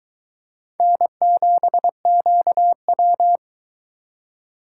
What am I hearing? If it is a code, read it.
Morse code, heard as a keyed tone, about 23 wpm, 700 Hz. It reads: N7QW